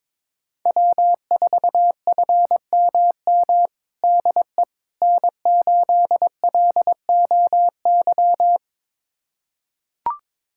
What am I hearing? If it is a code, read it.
Morse code, heard as a keyed tone, about 21 wpm, 700 Hz. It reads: W4FMM DE N8LOY
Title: USA Callsign DE Callsign - 5 Characters - 22wpm